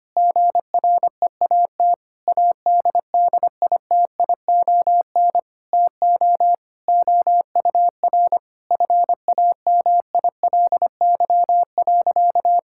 Morse code, heard as a keyed tone, about 25 wpm, 700 Hz. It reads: GREAT ADDITION TO OUR FAMILY.
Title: Sentences from Top 1000 Words - Review Part 6 - 25wpm